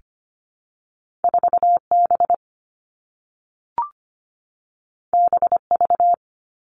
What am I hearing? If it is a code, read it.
Morse code, heard as a keyed tone, about 25 wpm, 700 Hz. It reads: 4B BV
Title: All Cut Numbers - 2-Digit 25wpm